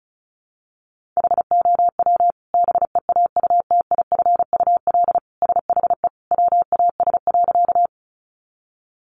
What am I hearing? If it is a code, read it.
Morse code, heard as a keyed tone, about 34 wpm, 700 Hz. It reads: HOW BEAUTIFUL SHE WAS.
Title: Sentences from Top 900 Words 35wpm